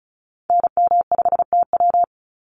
Morse code, heard as a keyed tone, about 34 wpm, 700 Hz. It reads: NM5TW
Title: NA SOTA Activator Callsigns 35wpm